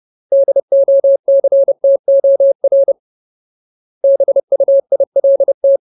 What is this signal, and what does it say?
Morse code, heard as a keyed tone, about 30 wpm, 550 Hz. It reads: DOCTOR BUILT
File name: Ultimate Word List / 30WPM Ultimate